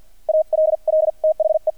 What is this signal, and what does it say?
Morse code, heard as a keyed tone, about 50 wpm, spaced as for 43 wpm, 630 Hz. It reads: ALL THE
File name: archived / CRSnet